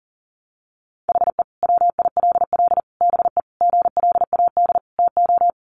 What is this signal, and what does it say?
Morse code, heard as a keyed tone, about 40 wpm, 700 Hz. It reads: HE WILL BE GLAD TO
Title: Sentences from Top 800 Words 40wpm